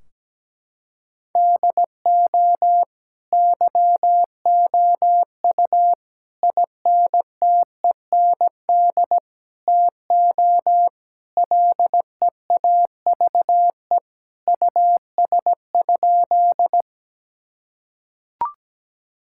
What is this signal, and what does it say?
Morse code, heard as a keyed tone, about 17 wpm, 700 Hz. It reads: DO YOU INTEND TO LEAVE US?
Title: Sentences from Top 1600 Words 17wpm